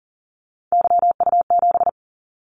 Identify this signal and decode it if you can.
Morse code, heard as a keyed tone, about 42 wpm, 700 Hz. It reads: YU7
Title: Callsign Prefixes for Top 25 DXCC - 40wpm